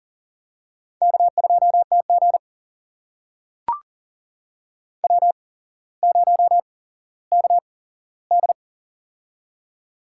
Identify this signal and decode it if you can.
Morse code, heard as a keyed tone, about 40 wpm, spaced as for 17 wpm, 700 Hz. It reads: K2TG W0KD
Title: USA Call Signs - 8x Character Spacing - 40wpm